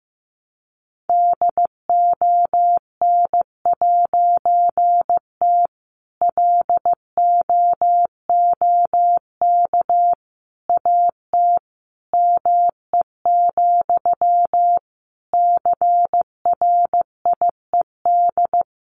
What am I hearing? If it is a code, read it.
Morse code, heard as a keyed tone, about 14 wpm, 700 Hz. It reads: DON'T LOOK AT ME, CRIED
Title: A Sound Of Thunder 15wpm